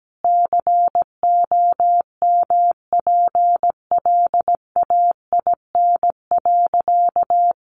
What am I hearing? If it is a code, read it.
Morse code, heard as a keyed tone, about 17 wpm, 700 Hz. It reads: COMPLAIN.
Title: Sentences from Top 1300 Words 17wpm